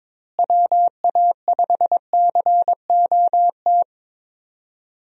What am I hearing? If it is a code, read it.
Morse code, heard as a keyed tone, about 22 wpm, 700 Hz. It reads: WA5COT
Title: USA Call Signs - 6 Characters - Repeated 3x - 22wpm